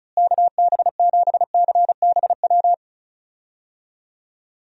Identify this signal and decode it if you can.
Morse code, heard as a keyed tone, about 34 wpm, 700 Hz. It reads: KB7CBW
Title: USA Call Signs - 6 Characters - Repeated 3x - 35wpm